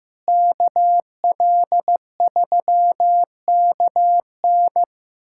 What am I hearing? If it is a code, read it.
Morse code, heard as a keyed tone, about 15 wpm, 700 Hz. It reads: KL3KN
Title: USA Callsign DE Callsign - 5 Characters - 15wpm